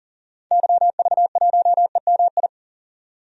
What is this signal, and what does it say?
Morse code, heard as a keyed tone, about 40 wpm, 700 Hz. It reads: YV1EMI
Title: DXCC-100 Call Signs - 6 Characters - 40wpm